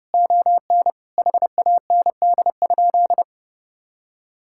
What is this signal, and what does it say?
Morse code, heard as a keyed tone, about 30 wpm, 700 Hz. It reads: ON HAND?
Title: Sentences from Top 1000 Words - Review Part 5 - 30wpm